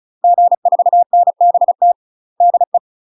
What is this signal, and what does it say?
Morse code, heard as a keyed tone, about 36 wpm, 700 Hz. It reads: G4NBT DE
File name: UK MASTER QSOs TRAINING / 35WPM / UK MASTER QSOs 0012